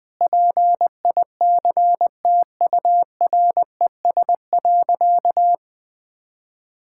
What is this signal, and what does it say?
Morse code, heard as a keyed tone, about 21 wpm, 700 Hz. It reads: PICTURES.
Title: Sentences with 8-Word Phrases - Spoken Beginning - 20wpm